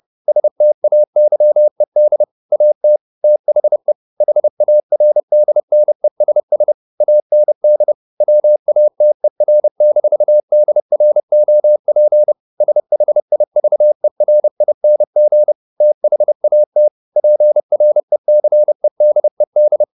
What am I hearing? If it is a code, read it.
Morse code, heard as a keyed tone, about 30 wpm, 600 Hz. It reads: STAYED AT THE HARDNESS AND WATER-DROP SHIVERING THAT PRECEDED